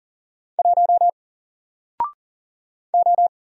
Morse code, heard as a keyed tone, about 40 wpm, 700 Hz. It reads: 1 O
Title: Single Letter-Number Alt 40wpm